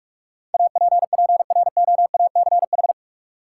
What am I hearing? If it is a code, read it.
Morse code, heard as a keyed tone, about 41 wpm, 700 Hz. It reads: APPROACH
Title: Sentences from Top 1400 Words - New Word Drill Alt - 45wpm